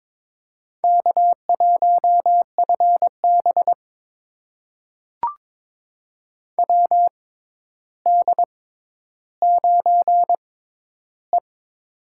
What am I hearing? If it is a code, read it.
Morse code, heard as a keyed tone, about 21 wpm, spaced as for 11 wpm, 700 Hz. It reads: K1FB WD9E
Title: USA Call Signs - 4 Characters - 6x Character Spacing - 22wpm